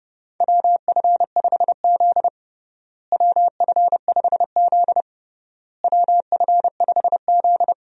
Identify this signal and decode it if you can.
Morse code, heard as a keyed tone, about 30 wpm, 700 Hz. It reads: WF5Z WF5Z WF5Z
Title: USA Call Signs - 4 Characters - Repeated 3x - 30wpm